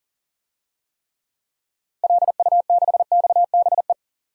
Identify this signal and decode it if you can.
Morse code, heard as a keyed tone, about 40 wpm, 700 Hz. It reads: LU6XBE
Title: DXCC-100 Call Signs - 6 Characters - Repeated 2x - 40wpm